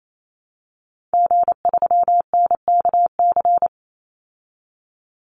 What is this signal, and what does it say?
Morse code, heard as a keyed tone, about 28 wpm, 700 Hz. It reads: G3NKC
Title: Call Signs Repeated 2x - 28wpm